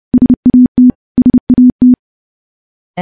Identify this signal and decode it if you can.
Morse code, heard as a keyed tone, about 30 wpm, 250 Hz. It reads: SAT SAT
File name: morse / morse-training-30WPM